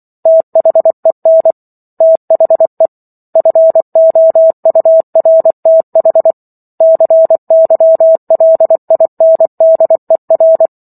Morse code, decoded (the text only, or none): THEN THE FOURTH CYLINDER